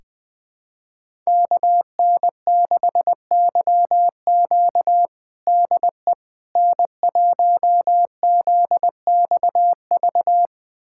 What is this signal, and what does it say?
Morse code, heard as a keyed tone, about 20 wpm, 700 Hz. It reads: KN6YQ DE N1ZXV
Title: USA Callsign DE Callsign - 5 Characters - 20wpm